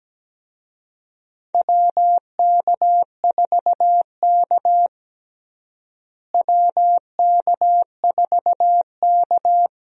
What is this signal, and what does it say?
Morse code, heard as a keyed tone, about 17 wpm, 700 Hz. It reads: WK4K WK4K
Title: USA Call Signs - 4 Characters - Repeated 3x - 17wpm